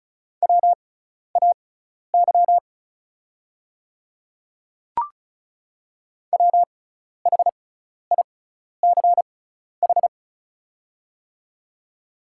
Morse code, heard as a keyed tone, about 34 wpm, spaced as for 12 wpm, 690 Hz. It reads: WAY WHICH
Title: 6x Character Spacing - Top 100 Words - 35wpm